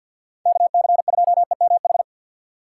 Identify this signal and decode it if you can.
Morse code, heard as a keyed tone, about 51 wpm, 700 Hz. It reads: KC2EMH
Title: USA Call Signs - 6 Characters - 4x Character Spacing - 50wpm